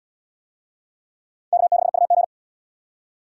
Morse code, heard as a keyed tone, about 74 wpm, 700 Hz. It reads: K6UG